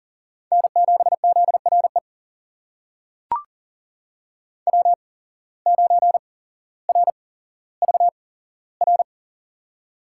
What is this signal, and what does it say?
Morse code, heard as a keyed tone, about 40 wpm, spaced as for 17 wpm, 700 Hz. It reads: N7ZRE W9RVR
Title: USA Call Signs - 5 Characters - 8x Character Spacing - 40wpm